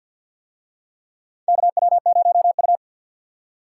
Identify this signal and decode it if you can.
Morse code, heard as a keyed tone, about 45 wpm, 700 Hz. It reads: KW0U
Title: USA Call Signs - 4 Characters - 50wpm